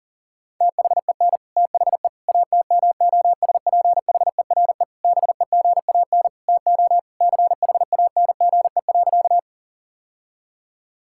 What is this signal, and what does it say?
Morse code, heard as a keyed tone, about 40 wpm, 700 Hz. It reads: THEN THE ATMOSPHERE BEGAN TO CHANGE.
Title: Sentences from Top 1600 Words 40wpm